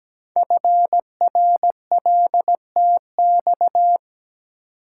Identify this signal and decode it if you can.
Morse code, heard as a keyed tone, about 17 wpm, 700 Hz. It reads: FRLTX
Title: Code Groups - 5 Letters - 17wpm